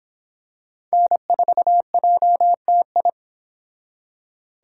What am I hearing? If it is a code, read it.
Morse code, heard as a keyed tone, about 24 wpm, 700 Hz. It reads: N4JTI